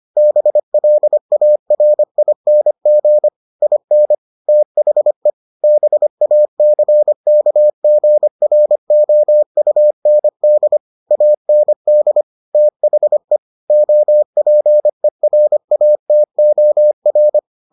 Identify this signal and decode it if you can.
Morse code, heard as a keyed tone, about 25 wpm, 600 Hz. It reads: BLARING IN THE BACKGROUND AND THE OPERATOR